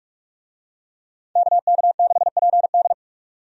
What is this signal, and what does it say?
Morse code, heard as a keyed tone, about 46 wpm, 700 Hz. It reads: KK6PD
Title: USA Call Signs - 5 Characters - 45wpm